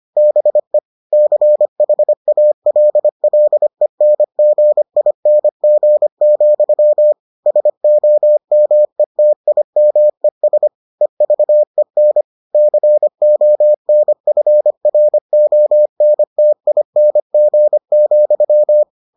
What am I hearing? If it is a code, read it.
Morse code, heard as a keyed tone, about 25 wpm, 600 Hz. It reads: BE CHALLENGING, SOMETIMES EVEN CONFRONTING,